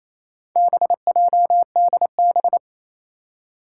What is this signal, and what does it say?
Morse code, heard as a keyed tone, about 28 wpm, 700 Hz. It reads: BJDB
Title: Four Challenging Letter-Number Combinations 28wpm